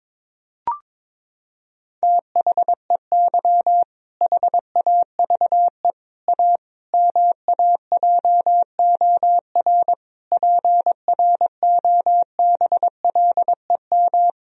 Morse code, heard as a keyed tone, about 21 wpm, 700 Hz. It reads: THEY HAVE A MAJOR PROBLEM
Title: Sentences from Top 1000 Words - Review Part 4 - 22wpm